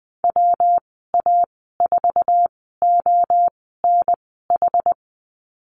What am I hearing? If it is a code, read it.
Morse code, heard as a keyed tone, about 20 wpm, spaced as for 14 wpm, 700 Hz. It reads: WA4ONH
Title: 193 Lesson 31g - Letter v - Call Signs 20wpm